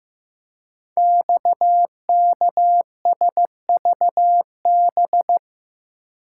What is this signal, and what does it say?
Morse code, heard as a keyed tone, about 15 wpm, 700 Hz. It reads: XKSVB
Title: Code Groups - 5 Letters - 15wpm